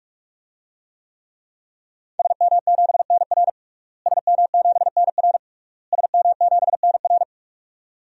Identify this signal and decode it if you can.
Morse code, heard as a keyed tone, about 46 wpm, 700 Hz. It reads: SM7NR SM7NR SM7NR
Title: DXCC-100 Call Signs - 5 Characters - Repeated 3x - 45wpm